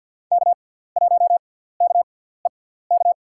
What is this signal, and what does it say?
Morse code, heard as a keyed tone, about 50 wpm, spaced as for 17 wpm, 700 Hz. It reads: K1KEK
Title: USA Call Signs - 5 Characters - 6x Character Spacing - 50wpm